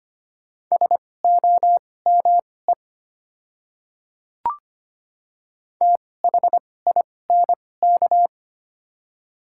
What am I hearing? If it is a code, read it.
Morse code, heard as a keyed tone, about 25 wpm, spaced as for 18 wpm, 700 Hz. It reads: SOME THINK
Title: 2x Character Spacing - Top 200 Words - 25wpm